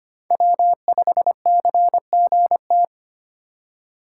Morse code, heard as a keyed tone, about 25 wpm, 700 Hz. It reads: W5CGT
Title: USA Call Signs - 5 Characters - 4x Character Spacing - 25wpm